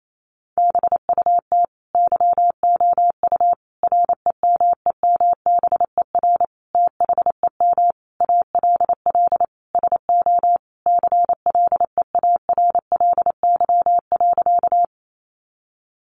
Morse code, heard as a keyed tone, about 28 wpm, 700 Hz. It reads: BUT YOU REMEMBER THEM ALL SO CLEARLY.
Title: Sentences from Top 1300 Words 28wpm